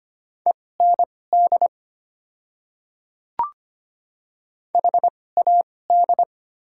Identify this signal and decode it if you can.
Morse code, heard as a keyed tone, about 25 wpm, spaced as for 18 wpm, 690 Hz. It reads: END HAD
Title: 2x Character Spacing - Top 200 Words - 25wpm